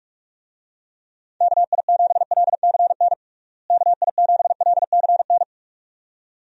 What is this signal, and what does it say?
Morse code, heard as a keyed tone, about 46 wpm, 700 Hz. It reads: KI7LCN KI7LCN
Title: USA Call Signs - 6 Characters - Repeated 2x - 45wpm